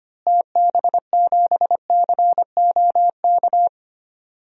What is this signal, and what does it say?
Morse code, heard as a keyed tone, about 25 wpm, 700 Hz. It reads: TB7COK
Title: DXCC-100 Call Signs - 6 Characters - Repeated 3x - 25wpm